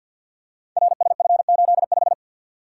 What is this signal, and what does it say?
Morse code, heard as a keyed tone, about 51 wpm, 700 Hz. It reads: RSF85
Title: Code Groups - 5 Letter:Numbers - 50wpm